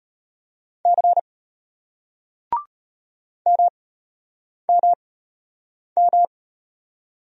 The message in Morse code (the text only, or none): C M M M